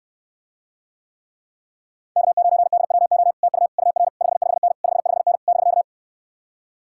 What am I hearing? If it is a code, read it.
Morse code, heard as a keyed tone, about 67 wpm, 700 Hz. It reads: K9DWO TU UR 55N 55N <BK>